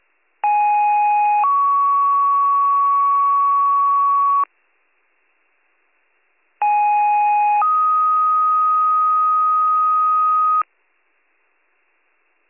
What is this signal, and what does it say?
Two-tone page, heard as a one-second tone, then a three-second tone, about 830 then 1120 Hz, about 830 then 1250 Hz. It is heard from 0.4 to 10.6 s.